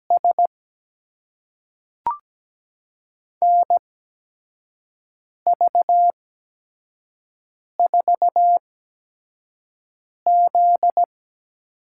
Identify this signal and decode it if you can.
Morse code, heard as a keyed tone, about 17 wpm, spaced as for 5 wpm, 700 Hz. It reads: S NV4Z
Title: USA Call Signs - 4 Characters - 8x Character Spacing - 17wpm